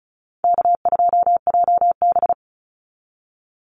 Morse code, heard as a keyed tone, about 34 wpm, 700 Hz. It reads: K2JB
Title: NA SOTA Activator Callsigns 35wpm